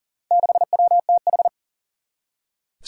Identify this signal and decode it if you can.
Morse code, heard as a keyed tone, about 37 wpm, 700 Hz. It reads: 6WTH
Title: Four Challenging Letter-Number Combinations 40wpm